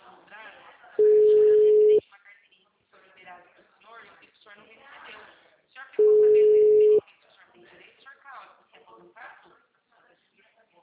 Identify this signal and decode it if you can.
Ringback tone (Europe, 425 Hz), with speech.